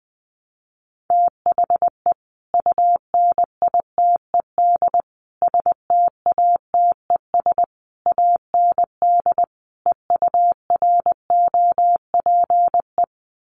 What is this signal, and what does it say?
Morse code, heard as a keyed tone, about 20 wpm, 700 Hz. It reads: THE UNITED STATES AND EUROPE
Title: Sets of 5 Words Encore 20wpm